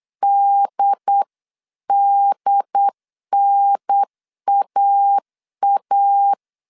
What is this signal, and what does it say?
Morse code, heard as a keyed tone, about 8 wpm, 800 Hz. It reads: DDNAA